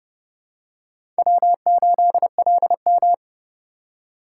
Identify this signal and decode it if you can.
Morse code, heard as a keyed tone, about 26 wpm, 700 Hz. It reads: W8LM